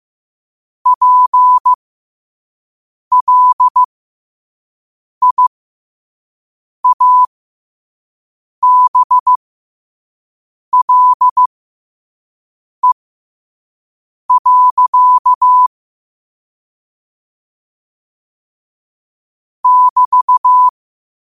Morse code, heard as a keyed tone, about 15 wpm, spaced as for 5 wpm, 1000 Hz. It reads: PLIABLE. <BT>
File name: qotdfiles / 5wpm